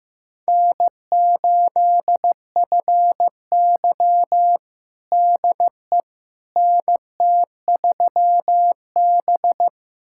Morse code, heard as a keyed tone, about 15 wpm, 700 Hz. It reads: N8FY DE NT3B
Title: USA Callsign DE Callsign - 4 Characters - 15wpm